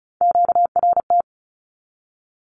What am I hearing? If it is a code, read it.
Morse code, heard as a keyed tone, about 36 wpm, 700 Hz. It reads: QRT
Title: Q-Codes with Spoken Meaning - 35wpm